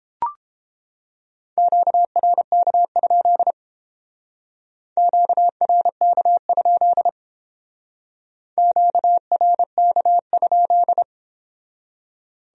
Morse code, heard as a keyed tone, about 31 wpm, 700 Hz. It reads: QRK? QRK? QRK?